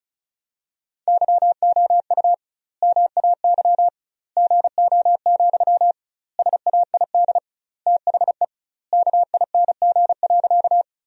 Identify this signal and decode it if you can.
Morse code, heard as a keyed tone, about 36 wpm, 700 Hz. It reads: YOU MAY GO, SAID THE KING.
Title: 2x Word-Spacing - Sentences from Top 400 Words - 35wpm